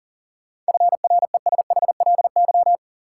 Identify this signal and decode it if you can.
Morse code, heard as a keyed tone, about 38 wpm, 700 Hz. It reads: FRESHLY
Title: Sets of 5 Words Encore - New Word Drill - 40wpm